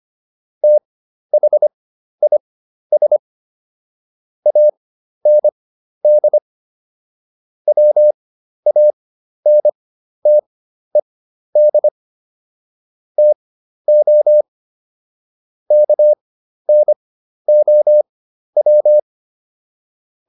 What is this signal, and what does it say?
Morse code, heard as a keyed tone, about 25 wpm, spaced as for 12 wpm, 600 Hz. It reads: THIS AND WANTED TO KNOW